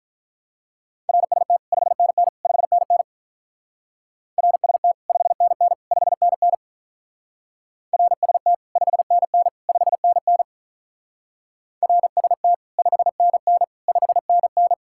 Morse code, heard as a keyed tone, about 42 wpm, 700 Hz. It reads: RST 5NN 5NN RST 5NN 5NN RST 5NN 5NN RST 5NN 5NN
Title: Speed-Racing Common QSO Phrases - 35wpm